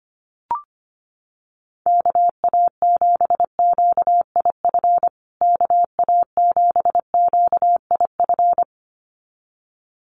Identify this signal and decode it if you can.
Morse code, heard as a keyed tone, about 25 wpm, 700 Hz. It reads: KA7QIF KA7QIF
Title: USA Call Signs Repeated 2x 25wpm